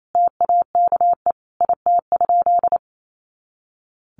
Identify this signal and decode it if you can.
Morse code, heard as a keyed tone, about 28 wpm, 700 Hz. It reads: TAKE IT?
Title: Sentences from Top 300 Words 28wpm